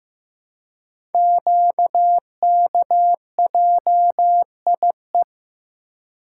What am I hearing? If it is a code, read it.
Morse code, heard as a keyed tone, about 15 wpm, 700 Hz. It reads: QKJIE
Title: Code Groups - 5 Letters - 15wpm